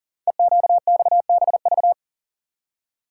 Morse code, heard as a keyed tone, about 41 wpm, 690 Hz. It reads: EQXBV
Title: Code Groups - 5 Letter:Numbers - 40wpm